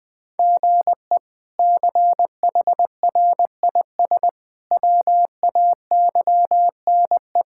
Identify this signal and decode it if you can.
Morse code, heard as a keyed tone, about 20 wpm, 700 Hz. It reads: GE CHRIS WAYNE
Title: K1USN SST QSO Practice - 20wpm